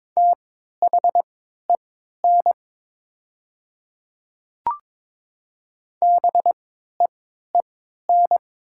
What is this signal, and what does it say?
Morse code, heard as a keyed tone, about 22 wpm, spaced as for 12 wpm, 700 Hz. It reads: THEN BEEN
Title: 3x Character Spacing - Top 100 Words - 22wpm